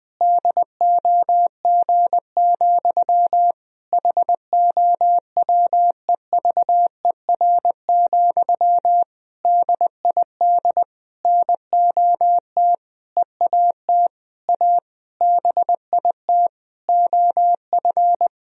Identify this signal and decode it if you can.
Morse code, heard as a keyed tone, about 20 wpm, 700 Hz. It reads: DOG, HOWEVER, DID NOT EAT A BIT OF
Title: Sentences from Top 1000 Words - Review Part 3 - 20wpm